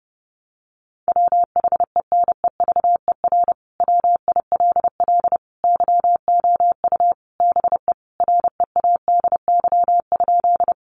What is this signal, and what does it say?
Morse code, heard as a keyed tone, about 28 wpm, 700 Hz. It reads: WHENEVER WILL YOU BE READY?
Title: Sentences from Top 1200 Words 30wpm